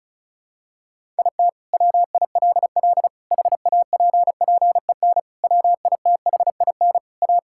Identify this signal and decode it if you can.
Morse code, heard as a keyed tone, about 36 wpm, 700 Hz. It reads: IT WILL HAPPEN WITHIN A
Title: Sentences from Top 1000 Words - Review Part 5 - 35wpm